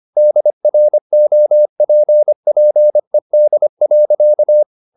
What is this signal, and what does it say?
Morse code, heard as a keyed tone, about 25 wpm, 600 Hz. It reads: DROPPED.
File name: the old man and the sea part 1 25wpm